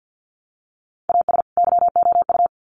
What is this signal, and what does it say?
Morse code, heard as a keyed tone, about 50 wpm, 700 Hz. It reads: AS YOU
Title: Sets of 2 Words 50wpm